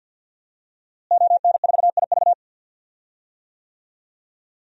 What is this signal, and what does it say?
Morse code, heard as a keyed tone, about 52 wpm, 700 Hz. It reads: ON4IV
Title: DXCC-100 Call Signs - 5 Characters - Repeated 3x - 50wpm